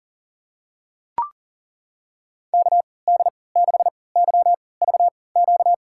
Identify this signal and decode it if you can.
Morse code, heard as a keyed tone, about 37 wpm, spaced as for 22 wpm, 700 Hz. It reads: KD6YVQ
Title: USA Call Signs - 6 Characters - 3x Character Spacing - 40wpm